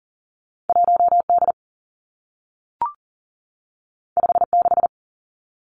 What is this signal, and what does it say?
Morse code, heard as a keyed tone, about 40 wpm, 700 Hz. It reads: 1 D 5 6
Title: Two Challenging Letter-Number Combinations 40wpm